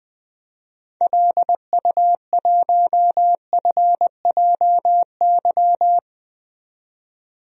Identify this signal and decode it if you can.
Morse code, heard as a keyed tone, about 20 wpm, 700 Hz. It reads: LU1FJY
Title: DXCC-100 Call Signs - 6 Characters - 20wpm